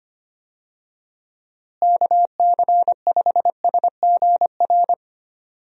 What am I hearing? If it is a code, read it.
Morse code, heard as a keyed tone, about 25 wpm, 700 Hz. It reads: KC5SGR
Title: USA Call Signs - 6 Characters - Repeated 3x - 25wpm